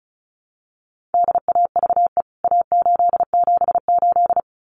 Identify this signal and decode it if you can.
Morse code, heard as a keyed tone, about 34 wpm, 700 Hz. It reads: DAVE A878
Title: CWT Exchange 35wpm